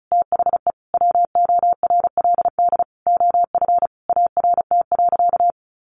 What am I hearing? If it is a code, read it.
Morse code, heard as a keyed tone, about 34 wpm, 700 Hz. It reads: THE WORLD OF ART.
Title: Sentences from Top 1500 Words 35wpm